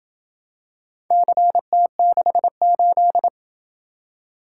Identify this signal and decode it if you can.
Morse code, heard as a keyed tone, about 27 wpm, 700 Hz. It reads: CT68